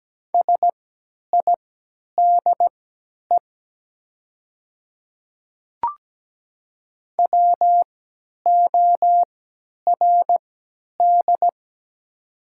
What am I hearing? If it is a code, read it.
Morse code, heard as a keyed tone, about 17 wpm, spaced as for 10 wpm, 700 Hz. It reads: SIDE WORD
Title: 3x Character Spacing - Top 300 Words - 17wpm